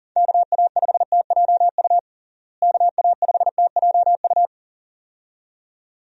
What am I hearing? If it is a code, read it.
Morse code, heard as a keyed tone, about 37 wpm, 700 Hz. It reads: KA5TJU KA5TJU
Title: USA Call Signs - 6 Characters - Repeated 2x - 40wpm